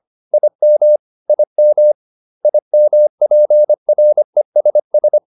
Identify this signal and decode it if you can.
Morse code, heard as a keyed tone, about 25 wpm, 600 Hz. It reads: IM IM IMPRESS